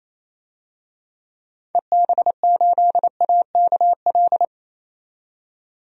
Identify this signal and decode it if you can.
Morse code, heard as a keyed tone, about 27 wpm, 700 Hz. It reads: EB8AKL
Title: DXCC-100 Call Signs - 6 Characters - 28wpm